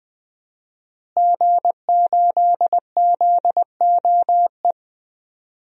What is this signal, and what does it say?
Morse code, heard as a keyed tone, about 20 wpm, 700 Hz. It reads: G8ZOE
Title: DXCC-100 Call Signs - 5 Characters - Repeated 2x - 20wpm